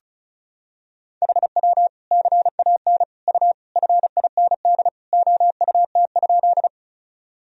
Morse code, heard as a keyed tone, about 34 wpm, 700 Hz. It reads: HW CAN U FIND OUT?
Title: Sentences from Top 200 Words and Abbreviations 35wpm